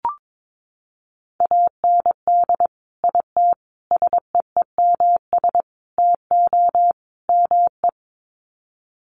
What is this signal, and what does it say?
Morse code, heard as a keyed tone, about 22 wpm, 700 Hz. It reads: AND IT SEEMS TO ME
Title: Sets of 5 Words Encore2 22wpm